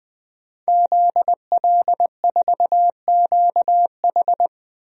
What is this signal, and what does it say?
Morse code, heard as a keyed tone, about 20 wpm, 700 Hz. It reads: ZL4QH
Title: DXCC-100 Call Signs - 5 Characters - 20wpm